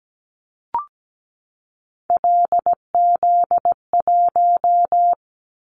Morse code, heard as a keyed tone, about 17 wpm, 700 Hz. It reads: LZ1
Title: Callsign Prefixes for Top 50 DXCC - 17wpm